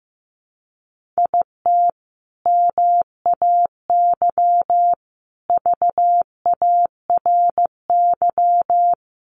Morse code, heard as a keyed tone, about 15 wpm, 700 Hz. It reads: IT MAY VARY
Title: Sentences from Top 1200 Words 15wpm